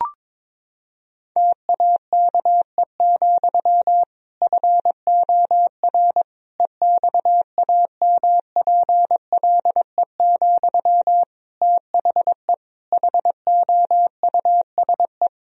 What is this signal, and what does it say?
Morse code, heard as a keyed tone, about 22 wpm, 700 Hz. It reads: TAKE, FOR EXAMPLE, THE HOUSE
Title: Sentences from Top 1000 Words - Review Part 1 - 22wpm